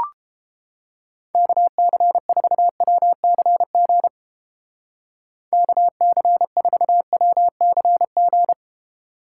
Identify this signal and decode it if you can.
Morse code, heard as a keyed tone, about 31 wpm, 700 Hz. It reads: KC4WCG KC4WCG